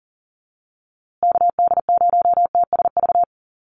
Morse code, heard as a keyed tone, about 37 wpm, 700 Hz. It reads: KD0TSV
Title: USA Call Signs Repeated 2x 40wpm